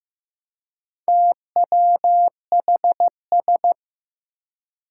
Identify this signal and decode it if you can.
Morse code, heard as a keyed tone, about 15 wpm, 700 Hz. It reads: TWHS
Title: Four Challenging Letter-Number Combinations 15wpm